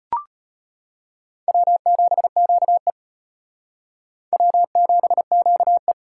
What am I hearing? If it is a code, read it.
Morse code, heard as a keyed tone, about 34 wpm, 700 Hz. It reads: W7QE W7QE